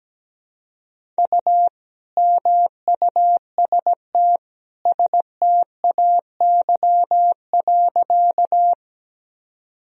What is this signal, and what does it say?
Morse code, heard as a keyed tone, about 17 wpm, 700 Hz. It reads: U MUST STAY.
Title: Sentences from Top 500 Words and Abbreviations 17wpm